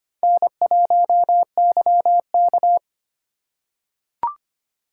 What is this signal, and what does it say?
Morse code, heard as a keyed tone, about 25 wpm, 700 Hz. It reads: N1YK
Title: USA Call Signs - 2x Character Spacing - 25wpm